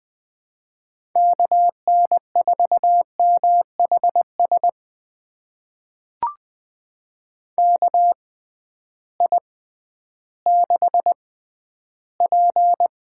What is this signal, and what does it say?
Morse code, heard as a keyed tone, about 20 wpm, spaced as for 12 wpm, 700 Hz. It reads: KN4MHS KI6P
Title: USA Call Signs - 6 Characters - 6x Character Spacing - 20wpm